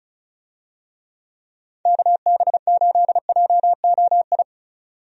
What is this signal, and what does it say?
Morse code, heard as a keyed tone, about 36 wpm, 700 Hz. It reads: KB8JOI
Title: USA Call Signs - 6 Characters - 35wpm